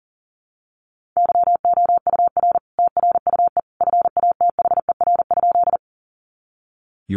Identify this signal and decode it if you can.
Morse code, heard as a keyed tone, about 40 wpm, 700 Hz. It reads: YOUR TRUE FATHER?
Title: Sentences from Top 400 Words 40wpm